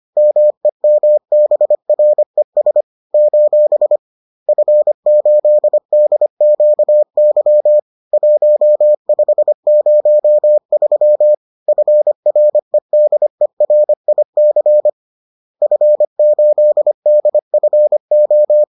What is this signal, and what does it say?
Morse code, heard as a keyed tone, about 25 wpm, 600 Hz. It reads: MEMBRES : F8DQY 1503 FREDERIC F8DFO
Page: bulletin UFT sous forme texte et en CW